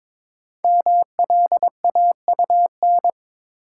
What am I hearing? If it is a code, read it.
Morse code, heard as a keyed tone, about 22 wpm, 690 Hz. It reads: MLAUN
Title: Code Groups - 5 Letters - 22wpm